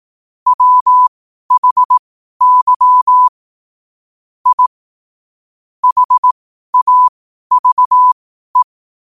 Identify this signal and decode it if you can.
Morse code, heard as a keyed tone, about 18 wpm, spaced as for 12 wpm, 1000 Hz. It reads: WHY I HAVE
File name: qotdfiles / 12wpm